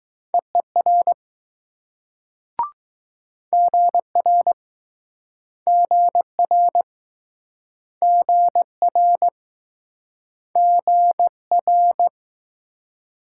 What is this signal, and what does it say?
Morse code, heard as a keyed tone, about 20 wpm, 700 Hz. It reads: EER GR GR GR GR